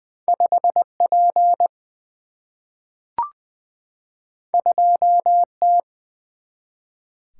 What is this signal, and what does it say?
Morse code, heard as a keyed tone, about 20 wpm, 700 Hz. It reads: 5 P 2 T
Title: Single-Letter Single-Number 20wpm